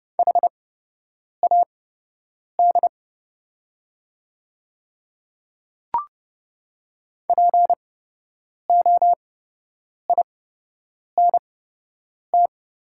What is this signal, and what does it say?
Morse code, heard as a keyed tone, about 30 wpm, spaced as for 8 wpm, 700 Hz. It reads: HAD POINT
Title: 8x Character Spacing - Top 200 Words - 30wpm